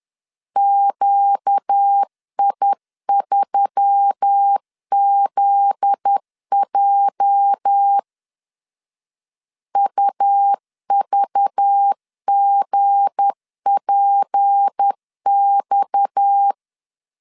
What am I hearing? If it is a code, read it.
Morse code, heard as a keyed tone, about 10 wpm, 800 Hz. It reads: QI3ZJ UVGPX
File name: LEZIONI / 8 YJ45 / 8 YJ 10wpm 2651